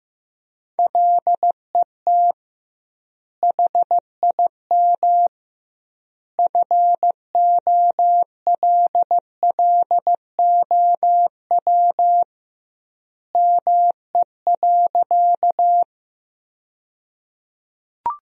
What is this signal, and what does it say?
Morse code, heard as a keyed tone, about 15 wpm, 700 Hz. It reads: LET HIM FOLLOW ME.
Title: 2x Word-Spacing - Sentences from Top 300 Words - 15wpm